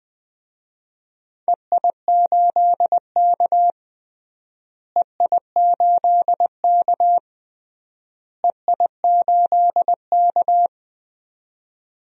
Morse code, heard as a keyed tone, about 20 wpm, 700 Hz. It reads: EI8K EI8K EI8K
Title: DXCC-100 Call Signs - 4 Characters - Repeated 3x - 20wpm